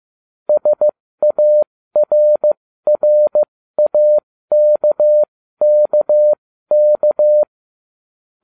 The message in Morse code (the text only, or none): SARRAKKK